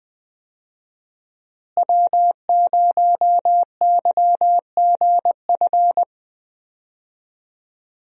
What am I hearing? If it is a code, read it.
Morse code, heard as a keyed tone, about 20 wpm, 700 Hz. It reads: W0YGF
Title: USA Call Signs - 5 Characters - Repeated 2x - 20wpm